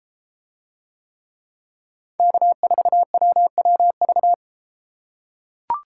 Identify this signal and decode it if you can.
Morse code, heard as a keyed tone, about 31 wpm, 700 Hz. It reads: K4WWV